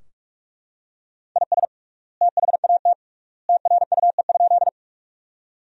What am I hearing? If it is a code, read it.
Morse code, heard as a keyed tone, about 41 wpm, 700 Hz. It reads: IS THAT TRUE?
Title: 3x Word-Spacing - Sentences from Top 400 Words - 45wpm